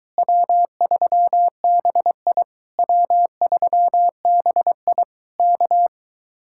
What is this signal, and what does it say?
Morse code, heard as a keyed tone, about 24 wpm, 700 Hz. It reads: W3BI W3BI K